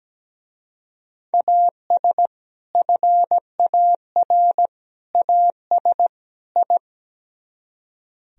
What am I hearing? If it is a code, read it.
Morse code, heard as a keyed tone, about 17 wpm, 700 Hz. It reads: AS FAR AS I
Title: Speed-Racing - Sets of 4 Words 17wpm